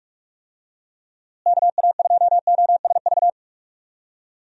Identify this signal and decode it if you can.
Morse code, heard as a keyed tone, about 41 wpm, 700 Hz. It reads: KA2OSV
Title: USA Call Signs - 6 Characters - Repeated 2x - 45wpm